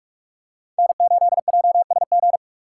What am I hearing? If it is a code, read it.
Morse code, heard as a keyed tone, about 46 wpm, 700 Hz. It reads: N8JSG
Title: USA Call Signs - 5 Characters - 3x Character Spacing - 45wpm